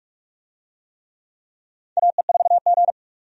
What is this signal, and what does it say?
Morse code, heard as a keyed tone, about 46 wpm, 700 Hz. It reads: AE4G
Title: USA Call Signs - 4 Characters - 45wpm